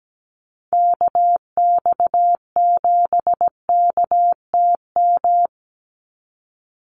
Morse code, heard as a keyed tone, about 17 wpm, 700 Hz. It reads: KX7KTM
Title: USA Call Signs 17wpm